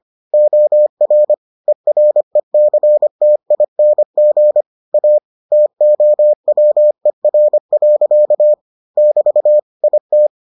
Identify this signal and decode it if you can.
Morse code, heard as a keyed tone, about 25 wpm, 600 Hz. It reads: OR ERECTING A TOWER. <BT> IT